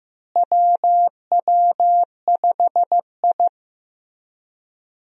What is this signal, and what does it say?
Morse code, heard as a keyed tone, about 15 wpm, 700 Hz. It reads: WW5I
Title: USA Call Signs - 4 Characters - Repeated 3x - 15wpm